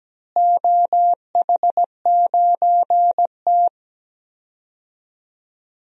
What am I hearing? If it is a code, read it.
Morse code, heard as a keyed tone, about 17 wpm, 700 Hz. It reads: OH9T
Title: DXCC-100 Call Signs - 4 Characters - Repeated 2x - 17wpm